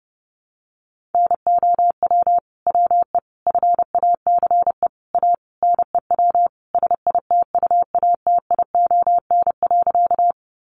Morse code, heard as a keyed tone, about 30 wpm, 700 Hz. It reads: NOW WE FACE A NEW SITUATION.
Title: Sentences from Top 1200 Words 30wpm